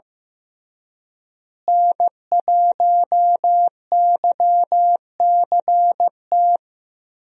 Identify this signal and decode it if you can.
Morse code, heard as a keyed tone, about 15 wpm, 700 Hz. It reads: N1YCT
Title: USA Call Signs - 5 Characters - Repeated 3x - 15wpm